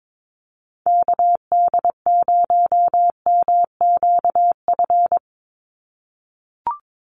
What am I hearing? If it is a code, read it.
Morse code, heard as a keyed tone, about 22 wpm, 700 Hz. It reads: KD0MQF
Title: USA Call Signs 22wpm